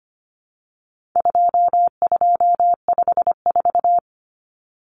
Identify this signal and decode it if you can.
Morse code, heard as a keyed tone, about 25 wpm, 700 Hz. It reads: 2254
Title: Common Cut Numbers - 4-Digit 25wpm